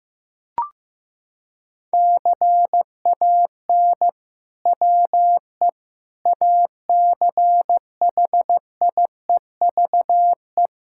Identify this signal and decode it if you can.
Morse code, heard as a keyed tone, about 16 wpm, 700 Hz. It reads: CAN WE ACHIEVE
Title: Sentences from Top 1000 Words - Review Part 6 - 15wpm